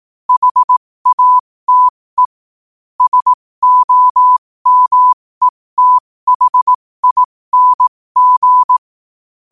Morse code, heard as a keyed tone, about 18 wpm, spaced as for 15 wpm, 1000 Hz. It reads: HATE SOMETHING
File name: qotdfiles / 15wpm